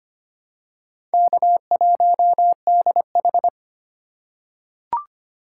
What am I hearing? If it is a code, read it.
Morse code, heard as a keyed tone, about 25 wpm, 700 Hz. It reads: K1DH
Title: USA Call Signs - 4 Characters - 2x Character Spacing - 25wpm